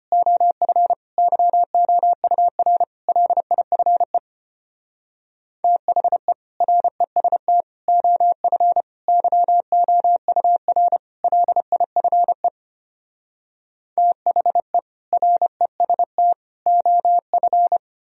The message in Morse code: OF YOUR LIFE THE REST OF YOUR LIFE THE REST OF

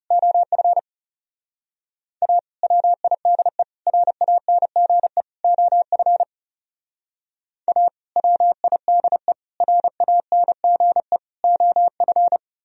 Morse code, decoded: OF A WIDE RANGE OF A WIDE RANGE OF

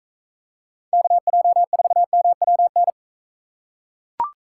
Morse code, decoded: KJ4MWN

